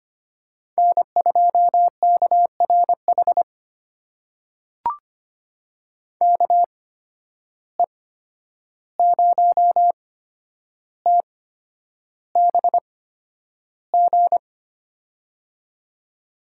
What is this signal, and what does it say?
Morse code, heard as a keyed tone, about 25 wpm, spaced as for 10 wpm, 700 Hz. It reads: N2KRH KE0TBG
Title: USA Call Signs - 8x Character Spacing - 25wpm